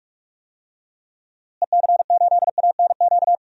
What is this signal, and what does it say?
Morse code, heard as a keyed tone, about 46 wpm, 700 Hz. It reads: EC8ANQ
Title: DXCC-100 Call Signs - 6 Characters - Repeated 2x - 45wpm